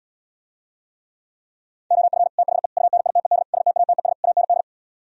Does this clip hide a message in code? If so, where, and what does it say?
Yes: Morse code from 1.9 to 4.6 s, heard as a keyed tone, about 77 wpm, 700 Hz. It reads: OF THE UNITED STATES AND